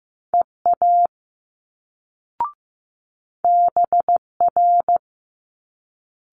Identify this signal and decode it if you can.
Morse code, heard as a keyed tone, about 15 wpm, 700 Hz. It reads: EA BR
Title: Multi-letter Phonemes Quick Fire 15wpm